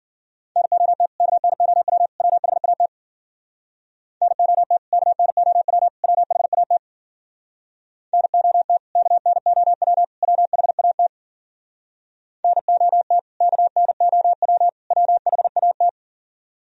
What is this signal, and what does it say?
Morse code, heard as a keyed tone, about 52 wpm, 700 Hz. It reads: NOT KNOW WHAT NOT KNOW WHAT NOT KNOW WHAT NOT KNOW WHAT